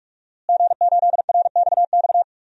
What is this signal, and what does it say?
Morse code, heard as a keyed tone, about 46 wpm, 700 Hz. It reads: G8RXX
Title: DXCC-100 Call Signs - 5 Characters - Repeated 2x - 45wpm